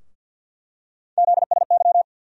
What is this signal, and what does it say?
Morse code, heard as a keyed tone, about 47 wpm, 700 Hz. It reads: ZSY
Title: USA Call Sign Postfixes - 3 Letters 50wpm